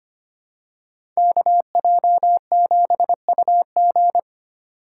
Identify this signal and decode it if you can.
Morse code, heard as a keyed tone, about 25 wpm, 700 Hz. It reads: KJ7UG
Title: USA Call Signs - 5 Characters - 2x Character Spacing - 25wpm